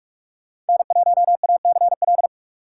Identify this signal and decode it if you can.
Morse code, heard as a keyed tone, about 46 wpm, 700 Hz. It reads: N1ACL
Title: USA Call Signs - 5 Characters - 3x Character Spacing - 45wpm